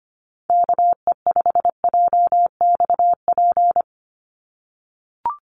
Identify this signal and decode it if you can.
Morse code, heard as a keyed tone, about 25 wpm, 700 Hz. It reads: KE5JXP
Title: USA Call Signs 25wpm